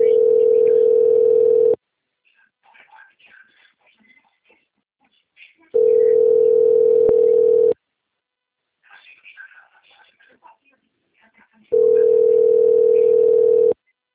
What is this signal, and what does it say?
Ringback tone (North America), 440 + 480 Hz, with speech.
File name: out- - - - - .82996